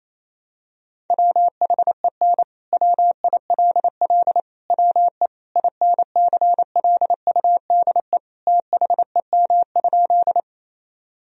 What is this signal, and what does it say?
Morse code, heard as a keyed tone, about 28 wpm, 700 Hz. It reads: WHEN WILL WE INCLUDE THEM?
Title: Sentences from Top 1000 Words - Review Part 3 - 28wpm